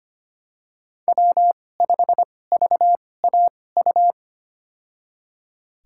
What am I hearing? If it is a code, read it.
Morse code, heard as a keyed tone, about 25 wpm, spaced as for 18 wpm, 700 Hz. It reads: W5VAU
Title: USA Call Signs - 5 Characters - 2x Character Spacing - 25wpm